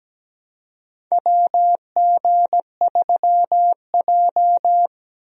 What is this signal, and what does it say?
Morse code, heard as a keyed tone, about 17 wpm, 700 Hz. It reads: WG3J
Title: USA Call Signs - 4 Characters - 17wpm